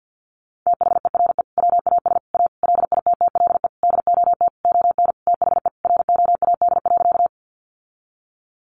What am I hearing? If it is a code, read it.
Morse code, heard as a keyed tone, about 50 wpm, 700 Hz. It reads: THERE WAS A LITTLE DOT ON THE ROAD.
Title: Sentences from Top 1500 Words 50wpm